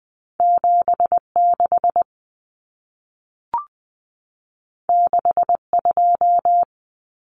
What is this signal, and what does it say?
Morse code, heard as a keyed tone, about 20 wpm, 700 Hz. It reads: 76 62
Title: Common Cut Numbers - 2-Digit 20wpm